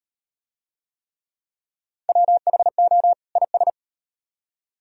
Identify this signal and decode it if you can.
Morse code, heard as a keyed tone, about 36 wpm, 700 Hz. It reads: WHO IS